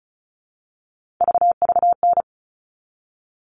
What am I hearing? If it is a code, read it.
Morse code, heard as a keyed tone, about 34 wpm, 700 Hz. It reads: VVN
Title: All Cut Numbers - 3-Digit 35wpm